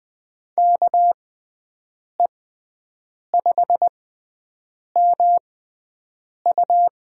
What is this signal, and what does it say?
Morse code, heard as a keyed tone, about 20 wpm, spaced as for 7 wpm, 700 Hz. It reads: KE5MU
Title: USA Call Signs - 6x Character Spacing - 20wpm